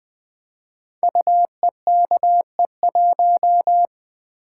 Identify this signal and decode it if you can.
Morse code, heard as a keyed tone, about 20 wpm, 700 Hz. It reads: UEKE1
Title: Code Groups - 5 Letter:Numbers - 20wpm